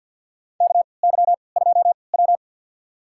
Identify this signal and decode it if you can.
Morse code, heard as a keyed tone, about 52 wpm, spaced as for 29 wpm, 700 Hz. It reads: KY2W
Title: USA Call Signs - 4 Characters - 3x Character Spacing - 50wpm